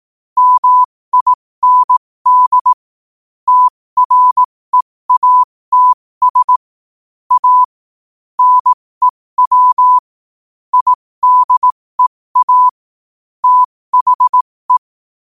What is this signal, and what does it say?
Morse code, heard as a keyed tone, about 18 wpm, spaced as for 15 wpm, 1000 Hz. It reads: MIND TREATS A NEW IDEA THE
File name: qotdfiles / 15wpm